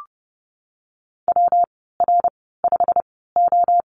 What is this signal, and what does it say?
Morse code, heard as a keyed tone, about 31 wpm, spaced as for 17 wpm, 700 Hz. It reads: WR5O
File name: Lesson 19g - Letter w - Call Signs 30wpm